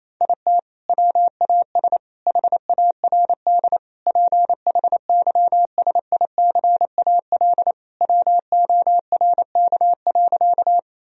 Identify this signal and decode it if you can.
Morse code, heard as a keyed tone, about 28 wpm, 700 Hz. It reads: IT WAS HARD PHYSICAL WORK.
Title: Sentences from Top 1000 Words - Review Part 4 - 28wpm